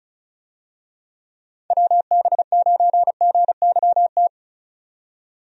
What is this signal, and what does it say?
Morse code, heard as a keyed tone, about 36 wpm, 700 Hz. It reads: WB9GYT
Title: USA Call Signs - 6 Characters - 35wpm